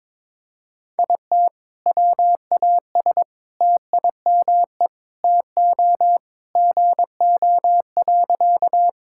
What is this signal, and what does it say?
Morse code, heard as a keyed tone, about 22 wpm, 700 Hz. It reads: IT WAS TIME TO GO.
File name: practice / 22wpm / Sentences with 4-Word Phrases - 22wpm